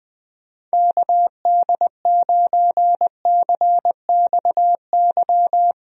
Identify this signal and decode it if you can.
Morse code, heard as a keyed tone, about 20 wpm, 700 Hz. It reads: KD9CXY
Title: USA Call Signs - 6 Characters - 20wpm